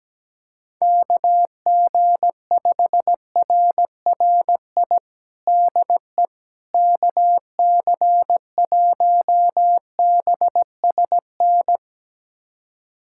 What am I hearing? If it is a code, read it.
Morse code, heard as a keyed tone, about 17 wpm, 700 Hz. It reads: KG5RRI DE KC1BSN
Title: USA Callsign DE Callsign 17wpm